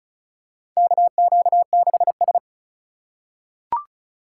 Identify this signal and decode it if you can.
Morse code, heard as a keyed tone, about 36 wpm, 700 Hz. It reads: KQ6S
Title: USA Call Signs - 4 Characters - 2x Character Spacing - 35wpm